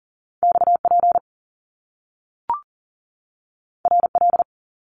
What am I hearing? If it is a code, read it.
Morse code, heard as a keyed tone, about 40 wpm, 700 Hz. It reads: XP RL
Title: Top Two Letter Combinations in Words 40wpm